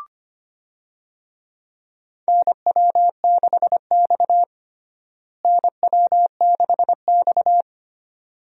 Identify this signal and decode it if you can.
Morse code, heard as a keyed tone, about 25 wpm, 700 Hz. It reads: NW6X NW6X
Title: USA Call Signs - 4 Characters - Repeated 2x - 25wpm